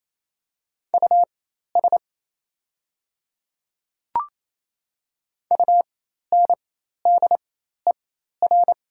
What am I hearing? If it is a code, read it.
Morse code, heard as a keyed tone, about 28 wpm, spaced as for 13 wpm, 700 Hz. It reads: US UNDER
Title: 4x Character Spacing - Top 200 Words - 28wpm